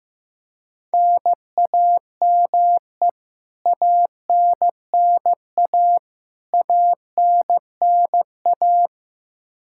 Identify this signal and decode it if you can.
Morse code, heard as a keyed tone, about 15 wpm, 700 Hz. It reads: NAME ANNA ANNA
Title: K1USN SST QSO Practice 15wpm